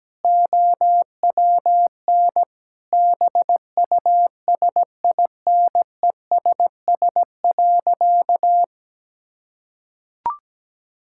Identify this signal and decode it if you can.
Morse code, heard as a keyed tone, about 17 wpm, 700 Hz. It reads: OWN BUSINESS.
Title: Sentences from Top 1000 Words - Review Part 6 - 17wpm